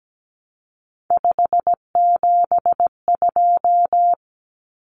Morse code, heard as a keyed tone, about 17 wpm, 700 Hz. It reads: 572
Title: Common Cut Numbers - 3-Digit 17wpm